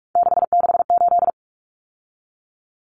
Morse code, heard as a keyed tone, about 44 wpm, 700 Hz. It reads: 668
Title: Numbers - 3-Digit 45wpm